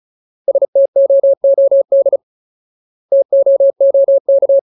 Morse code, heard as a keyed tone, about 34 wpm, 550 Hz. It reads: STOOD TOOK
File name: Ultimate Word List / 35WPM Ultimate